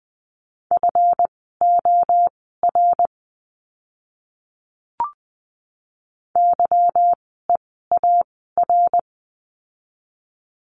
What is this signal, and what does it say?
Morse code, heard as a keyed tone, about 20 wpm, spaced as for 14 wpm, 700 Hz. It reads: FOR YEAR
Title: 206 Lesson 33f - Letter j - Words 20wpm